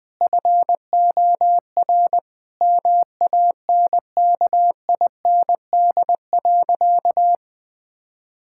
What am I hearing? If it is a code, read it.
Morse code, heard as a keyed tone, about 21 wpm, 700 Hz. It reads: FOR MANKIND.
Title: Sentences with 9-Word Phrases - 20wpm